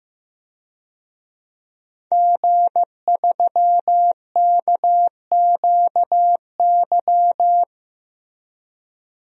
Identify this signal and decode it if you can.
Morse code, heard as a keyed tone, about 15 wpm, 700 Hz. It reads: G3KQY
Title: DXCC-100 Call Signs - 5 Characters - Repeated 3x - 15wpm